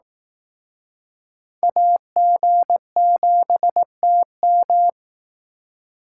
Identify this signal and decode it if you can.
Morse code, heard as a keyed tone, about 18 wpm, 700 Hz. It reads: AG7TM
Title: Speed Racing - USA 2x2 Call Signs 15wpm